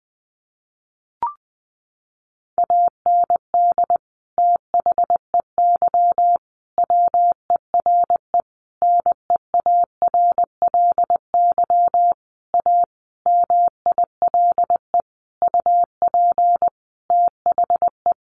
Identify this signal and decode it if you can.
Morse code, heard as a keyed tone, about 20 wpm, 700 Hz. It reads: AND THEY WERE NEARLY A MILE UP THE
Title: Sentences from Top 1300 Words 20wpm